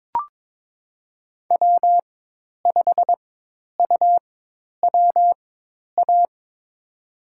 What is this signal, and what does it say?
Morse code, heard as a keyed tone, about 22 wpm, spaced as for 10 wpm, 700 Hz. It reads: W5UWA
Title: USA Call Signs - 5 Characters - 4x Character Spacing - 22wpm